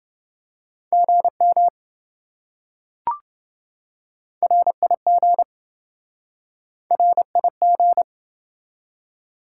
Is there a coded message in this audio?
Yes: Morse code from 0.9 to 8.0 s, heard as a keyed tone, about 27 wpm, 700 Hz. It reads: GM RIG RIG